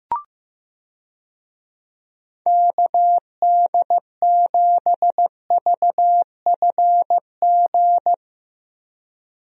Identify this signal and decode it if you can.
Morse code, heard as a keyed tone, about 15 wpm, 700 Hz. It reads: KD7VFG
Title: USA Call Signs - 6 Characters - Repeated 2x - 15wpm